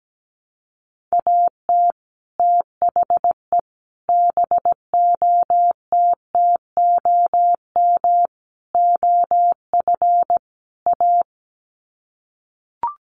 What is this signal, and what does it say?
Morse code, heard as a keyed tone, about 17 wpm, 700 Hz. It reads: AT THE BOTTOM OF A
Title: Sets of 5 Words Encore2 17wpm